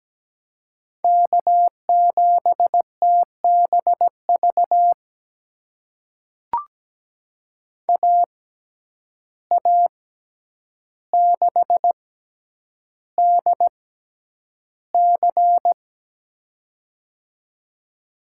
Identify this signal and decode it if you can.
Morse code, heard as a keyed tone, about 17 wpm, spaced as for 9 wpm, 700 Hz. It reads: K7TBV AA6DC
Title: USA Call Signs - 5 Characters - 6x Character Spacing - 17wpm